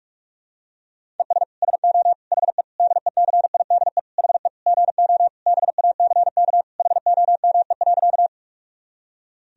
Morse code, heard as a keyed tone, about 41 wpm, 690 Hz. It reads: ES SO HE DECIDE HE GO BACK HOME.
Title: Sentences from Top 500 Words and Abbreviations 45wpm